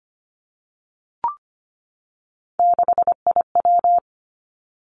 Morse code, heard as a keyed tone, about 25 wpm, 700 Hz. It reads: 6IW
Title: Three Challenging Letter-Number Combinations 25wpm